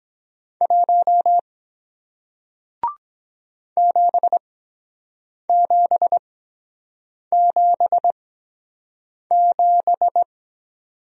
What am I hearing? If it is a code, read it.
Morse code, heard as a keyed tone, about 23 wpm, spaced as for 7 wpm, 700 Hz. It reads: 1 7777